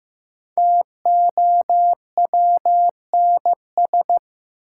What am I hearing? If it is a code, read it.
Morse code, heard as a keyed tone, about 15 wpm, 700 Hz. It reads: TOWNS
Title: Sentences from Top 1000 Words - Review Part 6 - New Word Drill - 15wpm